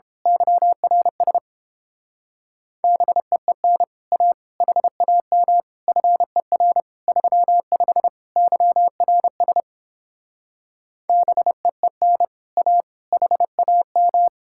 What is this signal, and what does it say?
Morse code, heard as a keyed tone, about 29 wpm, 700 Hz. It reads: YRS BEEN A HAM FER 35 YRS BEEN A HAM